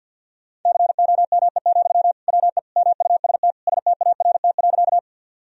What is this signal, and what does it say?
Morse code, heard as a keyed tone, about 52 wpm, 700 Hz. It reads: COME, WE MUST START.
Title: Sentences from Top 1000 Words - Review Part 1 - 50wpm